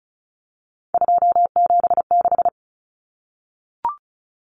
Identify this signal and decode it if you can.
Morse code, heard as a keyed tone, about 34 wpm, 700 Hz. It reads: 276
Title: Common Cut Numbers - 3-Digit 35wpm